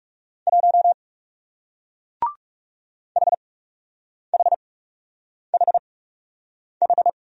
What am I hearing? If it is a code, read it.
Morse code, heard as a keyed tone, about 41 wpm, spaced as for 8 wpm, 700 Hz. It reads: 1 HHHH